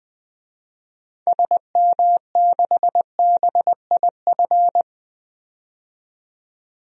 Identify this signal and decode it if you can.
Morse code, heard as a keyed tone, about 20 wpm, 700 Hz. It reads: SM6BIF
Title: DXCC-100 Call Signs - 6 Characters - 20wpm